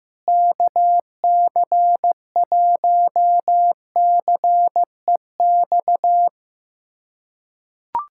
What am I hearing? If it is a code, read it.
Morse code, heard as a keyed tone, about 15 wpm, 700 Hz. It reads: KC1CEX
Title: USA Callsign DE Callsign - 6 Characters - 15wpm